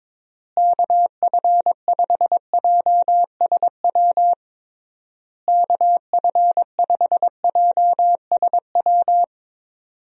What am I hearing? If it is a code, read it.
Morse code, heard as a keyed tone, about 22 wpm, 700 Hz. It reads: KF5JSW KF5JSW
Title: USA Call Signs - 6 Characters - Repeated 3x - 22wpm